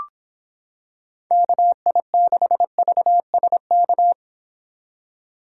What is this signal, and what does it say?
Morse code, heard as a keyed tone, about 25 wpm, 700 Hz. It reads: KI6VSK